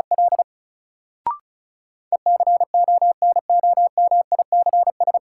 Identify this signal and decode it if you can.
Morse code, heard as a keyed tone, about 36 wpm, 700 Hz. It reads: L ECONOMICS
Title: Sentences from Top 1000 Words - Review Part 3 - New Word Drill - 35wpm